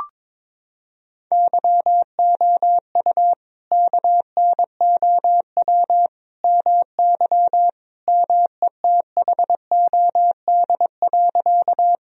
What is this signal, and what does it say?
Morse code, heard as a keyed tone, about 22 wpm, 700 Hz. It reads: YOU KNOW MY METHOD.
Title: Sentences from Top 1000 Words - Review Part 3 - 22wpm